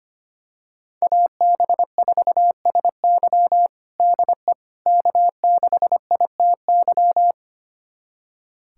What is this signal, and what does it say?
Morse code, heard as a keyed tone, about 25 wpm, 700 Hz. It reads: AB4SY DE K6ITY
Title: USA Callsign DE Callsign - 5 Characters - 25wpm